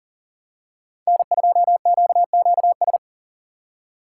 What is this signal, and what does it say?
Morse code, heard as a keyed tone, about 40 wpm, 700 Hz. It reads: N2QQS
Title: USA Call Signs - 5 Characters - 4x Character Spacing - 40wpm